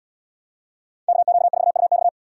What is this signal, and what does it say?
Morse code, heard as a keyed tone, about 72 wpm, 700 Hz. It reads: KQ4UX